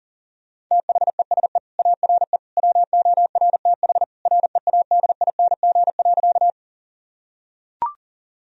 Morse code, heard as a keyed tone, about 40 wpm, 700 Hz. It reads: THESE ARE WORTH READING.
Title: Sentences from Top 1000 Words - Review Part 3 - 40wpm